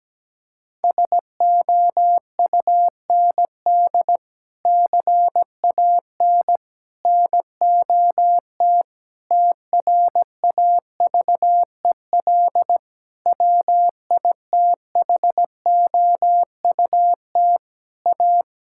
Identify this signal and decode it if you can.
Morse code, heard as a keyed tone, about 17 wpm, 700 Hz. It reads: SOUND CAN NOT TRAVEL WITHOUT A
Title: Sentences from Top 1000 Words - Review Part 6 - 17wpm